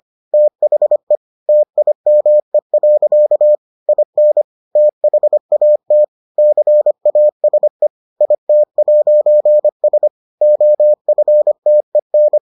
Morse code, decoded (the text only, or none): THE TIME. IN THAT CASE IT'S OFTEN